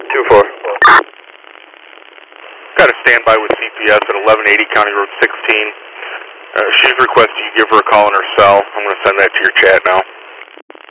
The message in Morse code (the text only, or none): none